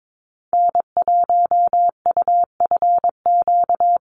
Morse code, heard as a keyed tone, about 22 wpm, 700 Hz. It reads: N1UFQ
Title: USA Call Signs Repeated 3x 22wpm